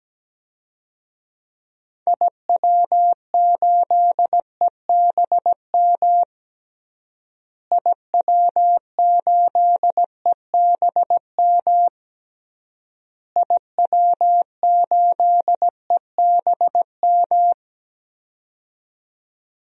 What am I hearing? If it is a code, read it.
Morse code, heard as a keyed tone, about 17 wpm, 700 Hz. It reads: IW8EBM IW8EBM IW8EBM
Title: DXCC-100 Call Signs - 6 Characters - Repeated 3x - 17wpm